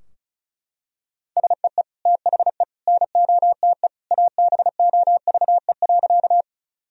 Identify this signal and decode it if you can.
Morse code, heard as a keyed tone, about 36 wpm, 700 Hz. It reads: SEE THE NOTE ABOVE.
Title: Sentences from Top 1000 Words - Review Part 2 - 35wpm